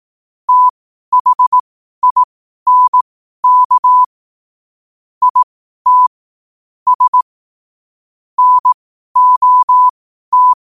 Morse code, heard as a keyed tone, about 18 wpm, spaced as for 12 wpm, 1000 Hz. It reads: THINK IT S NOT
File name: qotdfiles / 12wpm